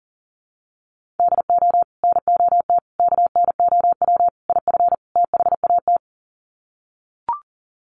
Morse code, heard as a keyed tone, about 40 wpm, 700 Hz. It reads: DO NOT KNOW IF THAT
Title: Sets of 5 Words Encore2 40wpm